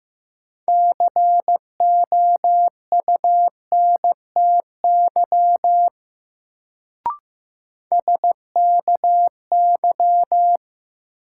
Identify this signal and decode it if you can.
Morse code, heard as a keyed tone, about 15 wpm, 700 Hz. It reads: COUNTY SKY
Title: Sentences from Top 1200 Words - New Word Drill Alt - 15wpm